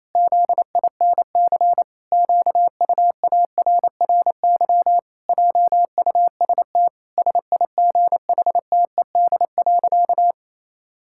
Morse code, heard as a keyed tone, about 28 wpm, 700 Hz. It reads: ZINC QUARRY JUST SIGHTED.
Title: Pangram Challenge v2 - 28wpm